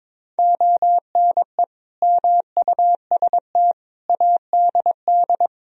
Morse code, heard as a keyed tone, about 22 wpm, 700 Hz. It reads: ONE MUST ADD
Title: Sentences from Top 1000 Words - Review Part 1 - 22wpm